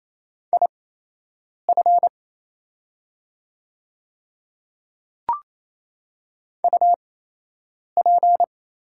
Morse code, heard as a keyed tone, about 28 wpm, spaced as for 8 wpm, 700 Hz. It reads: IF UP
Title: 8x Character Spacing - Top 100 Words - 28wpm